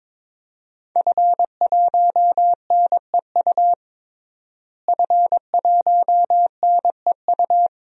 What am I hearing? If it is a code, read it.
Morse code, heard as a keyed tone, about 22 wpm, 700 Hz. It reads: F1NEU F1NEU
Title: DXCC-100 Call Signs - 5 Characters - Repeated 2x - 22wpm